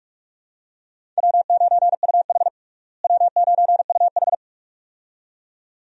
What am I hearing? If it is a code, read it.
Morse code, heard as a keyed tone, about 45 wpm, 700 Hz. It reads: W9UH W9UH
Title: USA Call Signs - 4 Characters - Repeated 2x - 45wpm